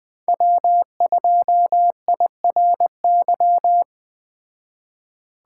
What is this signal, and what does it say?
Morse code, heard as a keyed tone, about 20 wpm, 700 Hz. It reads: W2IRY